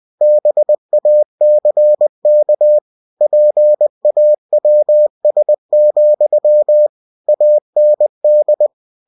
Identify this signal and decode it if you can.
Morse code, heard as a keyed tone, about 20 wpm, 600 Hz. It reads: BACK PAWS, AND
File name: cw-books / winnie-the-pooh / winnie-20wpm-page-0016